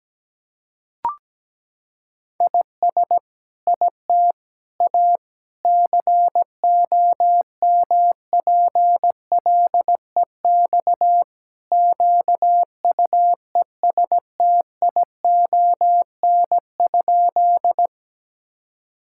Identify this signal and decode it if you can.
Morse code, heard as a keyed tone, about 17 wpm, 700 Hz. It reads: IS IT A COMPLEX QUESTION?
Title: Sentences from Top 1000 Words - Review Part 6 - 17wpm